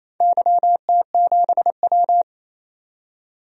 Morse code, heard as a keyed tone, about 28 wpm, 700 Hz. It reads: YT7W
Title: DXCC-100 Call Signs - 4 Characters - 28wpm